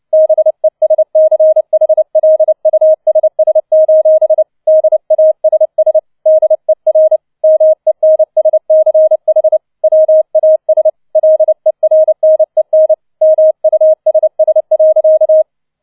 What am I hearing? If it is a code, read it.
Morse code, heard as a keyed tone, about 29 wpm, 630 Hz. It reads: BESCHLUSS: DASS DER MENSCH WAS LERNEN MUSS.
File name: 004 Vierter Streich 30wpm